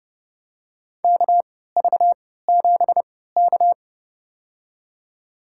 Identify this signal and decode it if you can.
Morse code, heard as a keyed tone, about 30 wpm, spaced as for 17 wpm, 700 Hz. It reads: KV7K
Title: USA Call Signs - 4 Characters - 3x Character Spacing - 30wpm